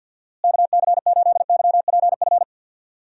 Morse code, heard as a keyed tone, about 47 wpm, 700 Hz. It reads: KC8YPF
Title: USA Call Signs - 6 Characters - Repeated 3x - 50wpm